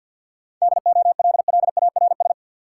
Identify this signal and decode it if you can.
Morse code, heard as a keyed tone, about 52 wpm, 700 Hz. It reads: DOLLARS
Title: Sets of 5 Words - New Word Drill - 50wpm